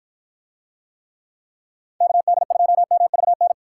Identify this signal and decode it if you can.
Morse code, heard as a keyed tone, about 48 wpm, 700 Hz. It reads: KD2MVN